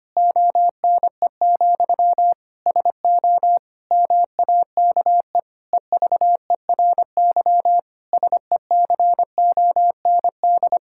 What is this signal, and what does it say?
Morse code, heard as a keyed tone, about 25 wpm, 700 Hz. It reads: ONE, SO MAKE EVERY SECOND
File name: practice / 25wpm / Difficult Sentences with Hints - 25wpm